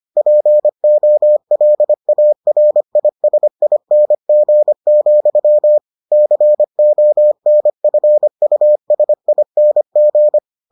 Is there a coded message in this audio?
Yes: Morse code from 0.2 to 10.4 s, heard as a keyed tone, about 25 wpm, 600 Hz. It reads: POLARISING, CONFUSING